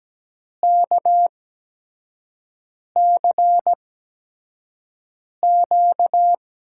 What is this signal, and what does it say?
Morse code, heard as a keyed tone, about 17 wpm, spaced as for 5 wpm, 700 Hz. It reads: KCQ
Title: USA Call Signs - 5 Characters - 8x Character Spacing - 17wpm